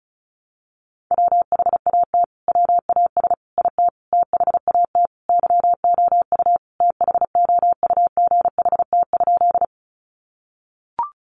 Morse code, heard as a keyed tone, about 34 wpm, 700 Hz. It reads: WHAT WAS IT THAT YOU THOUGHT?
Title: Sentences from Top 300 Words 35wpm